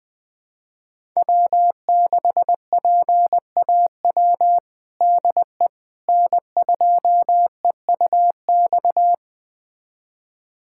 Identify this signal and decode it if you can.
Morse code, heard as a keyed tone, about 20 wpm, 700 Hz. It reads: W6PAW DE N2EUX
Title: USA Callsign DE Callsign - 5 Characters - 20wpm